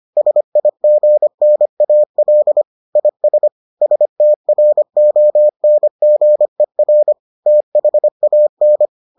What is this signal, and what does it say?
Morse code, heard as a keyed tone, about 25 wpm, 600 Hz. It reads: SIGNAL IS STRONGER THAN